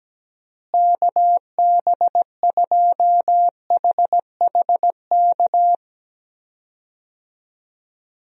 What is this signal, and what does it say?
Morse code, heard as a keyed tone, about 17 wpm, 700 Hz. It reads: KB2HHK
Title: USA Call Signs - 6 Characters - Repeated 2x - 17wpm